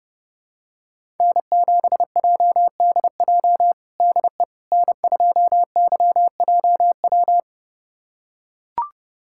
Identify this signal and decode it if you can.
Morse code, heard as a keyed tone, about 30 wpm, 700 Hz. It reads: N7JDJ DE N2YJW
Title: USA Callsign DE Callsign 30wpm